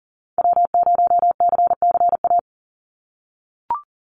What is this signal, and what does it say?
Morse code, heard as a keyed tone, about 40 wpm, 700 Hz. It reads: W0CCA
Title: NA SOTA Activator Callsigns 40wpm